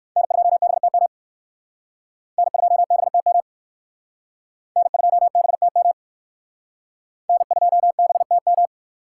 Morse code, heard as a keyed tone, about 56 wpm, 700 Hz. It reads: N2BTM N2BTM N2BTM N2BTM
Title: Speed Racing - USA Call Signs 45wpm